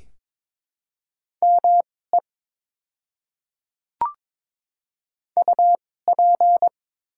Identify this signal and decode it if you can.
Morse code, heard as a keyed tone, about 22 wpm, spaced as for 16 wpm, 700 Hz. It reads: ME UP
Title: 2x Character Spacing - Top 200 Words - 22wpm